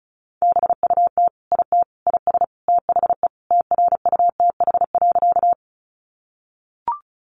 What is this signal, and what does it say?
Morse code, heard as a keyed tone, about 34 wpm, 700 Hz. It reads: BUT IT IS THE TRUTH.
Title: Sentences from Top 800 Words 35wpm